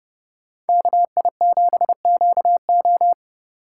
Morse code, heard as a keyed tone, about 36 wpm, 700 Hz. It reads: KI7QO
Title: Speed Racing - USA 2x2 Call Signs 30wpm